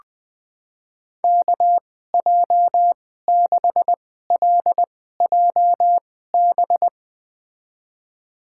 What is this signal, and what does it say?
Morse code, heard as a keyed tone, about 20 wpm, spaced as for 14 wpm, 700 Hz. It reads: KJ6LJB
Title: USA Call Signs - 6 Characters - 2x Character Spacing - 20wpm